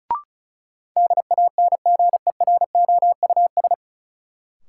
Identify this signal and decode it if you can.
Morse code, heard as a keyed tone, about 36 wpm, 700 Hz. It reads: DANGEROUS
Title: Sentences from Top 1200 Words - New Word Drill Alt - 35wpm